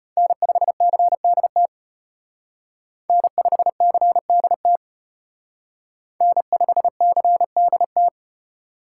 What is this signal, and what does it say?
Morse code, heard as a keyed tone, about 32 wpm, 700 Hz. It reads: N5CDT N5CDT N5CDT